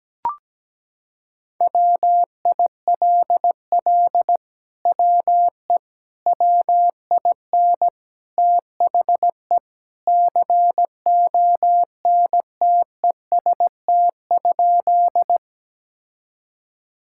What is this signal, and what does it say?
Morse code, heard as a keyed tone, about 17 wpm, 700 Hz. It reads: WILL WE WIN THE CONTEST?
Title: Sentences from Top 1600 Words 17wpm